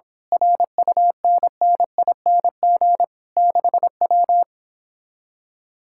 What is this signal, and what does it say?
Morse code, heard as a keyed tone, about 26 wpm, 700 Hz. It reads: RUNNING 6W
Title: Speed-Racing Common QSO Phrases Encore - 22wpm